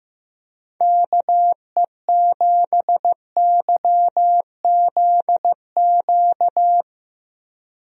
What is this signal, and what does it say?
Morse code, heard as a keyed tone, about 15 wpm, 700 Hz. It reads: KE7YZQ
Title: USA Call Signs - 6 Characters - Repeated 2x - 15wpm